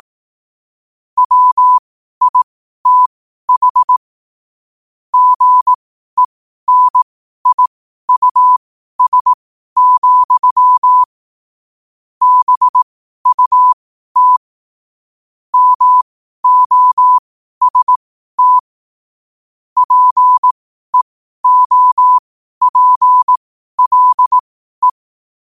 Morse code, heard as a keyed tone, about 18 wpm, spaced as for 12 wpm, 1000 Hz. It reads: WITH GENIUS, BUT MOST PEOPLE
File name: qotdfiles / 12wpm